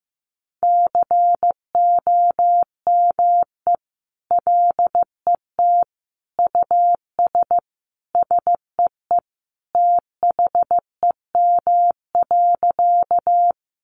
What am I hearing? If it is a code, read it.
Morse code, heard as a keyed tone, about 14 wpm, 700 Hz. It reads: COME LET US SEE THEM.
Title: Sentences from Top 300 Words 15wpm